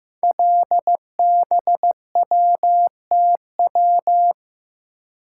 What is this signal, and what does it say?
Morse code, heard as a keyed tone, about 15 wpm, 700 Hz. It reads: LBWTW
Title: Code Groups - 5 Letter:Numbers - 15wpm